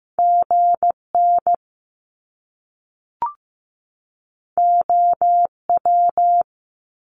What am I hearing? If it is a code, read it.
Morse code, heard as a keyed tone, about 15 wpm, 700 Hz. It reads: GN OW
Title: Top Two Letter Combinations in Words 15wpm